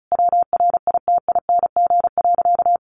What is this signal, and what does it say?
Morse code, heard as a keyed tone, about 34 wpm, 700 Hz. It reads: WRITING.
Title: Sentences from Top 800 Words 35wpm